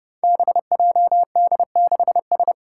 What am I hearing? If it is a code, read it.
Morse code, heard as a keyed tone, about 28 wpm, 700 Hz. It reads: BJD6S
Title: Five Challenging Letter-Number Combinations 30wpm